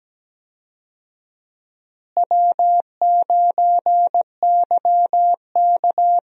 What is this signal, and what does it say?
Morse code, heard as a keyed tone, about 17 wpm, 700 Hz. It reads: W9YK
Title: USA Call Signs - 4 Characters - Repeated 3x - 17wpm